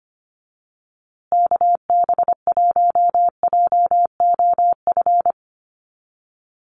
Morse code, heard as a keyed tone, about 25 wpm, 700 Hz. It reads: KB1JOF
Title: USA Call Signs Repeated 2x 25wpm